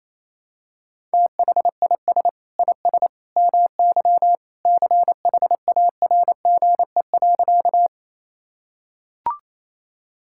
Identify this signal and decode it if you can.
Morse code, heard as a keyed tone, about 28 wpm, 700 Hz. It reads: THIS IS MY CHARGE.
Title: Sentences from Top 1000 Words - Review Part 6 - 28wpm